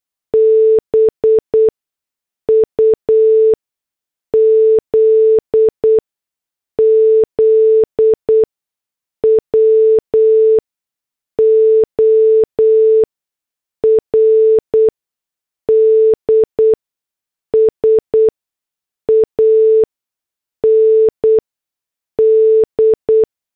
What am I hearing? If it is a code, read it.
Morse code, heard as a keyed tone, about 8 wpm, spaced as for 6 wpm, 440 Hz. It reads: BUZZWORDS AND